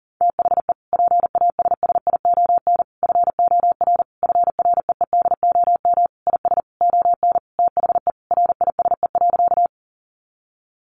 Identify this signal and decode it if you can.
Morse code, heard as a keyed tone, about 40 wpm, 700 Hz. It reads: THE PASSION FOR FREEDOM IS ON THE RISE.
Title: Sentences from Top 1300 Words 40wpm